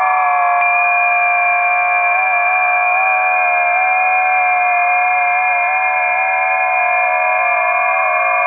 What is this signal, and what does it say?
FT8.
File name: tests / 20m busy / test 21